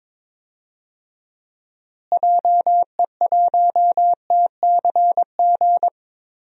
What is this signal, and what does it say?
Morse code, heard as a keyed tone, about 22 wpm, 700 Hz. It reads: JE1TCG
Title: DXCC-100 Call Signs - 6 Characters - Repeated 3x - 22wpm